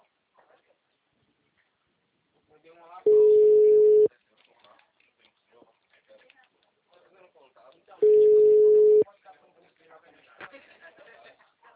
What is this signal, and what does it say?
Ringback tone (Europe, 425 Hz), with speech.